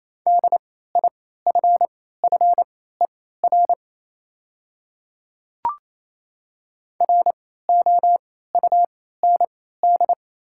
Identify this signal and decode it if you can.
Morse code, heard as a keyed tone, about 27 wpm, spaced as for 16 wpm, 700 Hz. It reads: DIFFER ROUND
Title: 3x Character Spacing - Top 200 Words - 28wpm